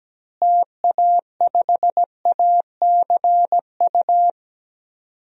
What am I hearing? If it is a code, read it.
Morse code, heard as a keyed tone, about 17 wpm, 700 Hz. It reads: TA5ACU
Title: DXCC-100 Call Signs - 6 Characters - Repeated 2x - 17wpm